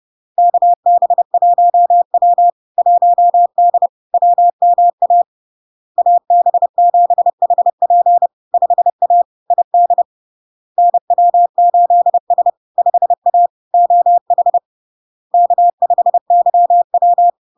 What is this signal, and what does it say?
Morse code, heard as a keyed tone, about 30 wpm, 700 Hz. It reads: KB1W 1D WMA AB7HP 5A ID NW8S 5A OH K5YW
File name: LARK-FD-2022-Log-in-CW-30wpm-700-Hz